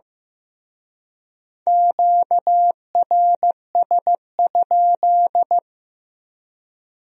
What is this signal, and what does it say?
Morse code, heard as a keyed tone, about 18 wpm, 700 Hz. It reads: QRS?
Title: Speed-Racing Q-Codes - 15wpm